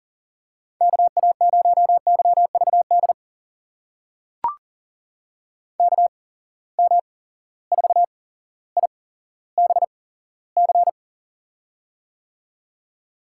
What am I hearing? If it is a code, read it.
Morse code, heard as a keyed tone, about 39 wpm, spaced as for 17 wpm, 700 Hz. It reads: KA0YVD KM4IBC
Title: USA Call Signs - 6 Characters - 8x Character Spacing - 40wpm